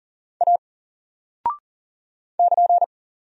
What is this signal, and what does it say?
Morse code, heard as a keyed tone, about 40 wpm, 700 Hz. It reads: A <KN>
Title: Single Character (Letter:Number:Punctuation:Prosign) 40wpm